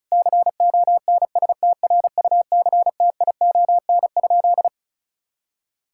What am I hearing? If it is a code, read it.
Morse code, heard as a keyed tone, about 34 wpm, 700 Hz. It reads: CONSTRUCTION?
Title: Sentences from Top 1600 Words 35wpm